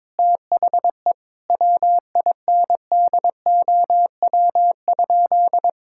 Morse code, heard as a keyed tone, about 22 wpm, 700 Hz. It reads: THE WINDOW?
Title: Sentences from Top 1600 Words 22wpm